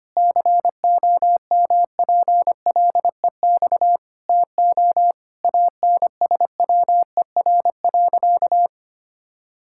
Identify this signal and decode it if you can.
Morse code, heard as a keyed tone, about 25 wpm, 700 Hz. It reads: COMPLEX TO ANSWER.
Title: Sentences from Top 1000 Words - Review Part 6 - 25wpm